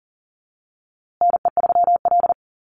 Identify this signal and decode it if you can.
Morse code, heard as a keyed tone, about 40 wpm, 700 Hz. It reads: NE3L
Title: USA Call Signs Repeated 3x 40wpm